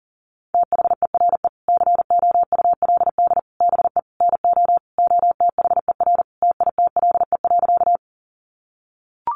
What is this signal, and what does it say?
Morse code, heard as a keyed tone, about 40 wpm, 700 Hz. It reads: THERE COULD BE NO OTHER TITLE.
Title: Sentences from Top 1200 Words 40wpm